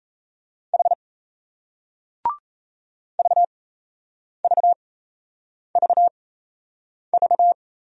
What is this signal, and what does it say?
Morse code, heard as a keyed tone, about 36 wpm, spaced as for 8 wpm, 700 Hz. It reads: H VVVV